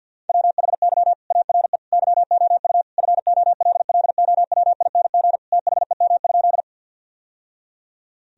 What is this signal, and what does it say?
Morse code, heard as a keyed tone, about 50 wpm, 700 Hz. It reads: WHY ARE YOU FOLLOWING THEM?
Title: Sentences from Top 1000 Words - Review Part 4 - 50wpm